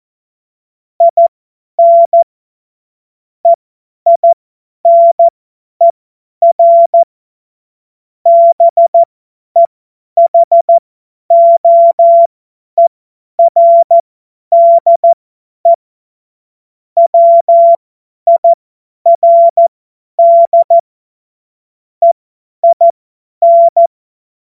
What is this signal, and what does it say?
Morse code, heard as a keyed tone, about 14 wpm, spaced as for 10 wpm, 680 Hz. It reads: IN EINER BEHOERDE WIRD EIN
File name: Witz-05-August-2025-50-F70